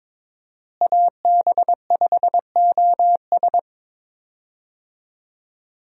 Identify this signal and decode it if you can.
Morse code, heard as a keyed tone, about 22 wpm, 700 Hz. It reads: AB5OS
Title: USA Call Signs - 5 Characters - Repeated 2x - 22wpm